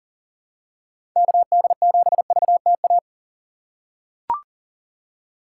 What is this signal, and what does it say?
Morse code, heard as a keyed tone, about 40 wpm, 700 Hz. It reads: KD7VTA
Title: USA Call Signs - 6 Characters - 3x Character Spacing - 40wpm